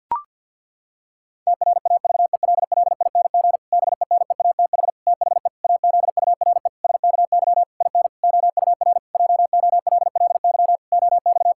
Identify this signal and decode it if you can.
Morse code, heard as a keyed tone, about 47 wpm, 700 Hz. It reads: TRAVELLING BENEATH THE AZURE SKY IN OUR JOLLY OX
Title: Pangram Challenge v2 - 50wpm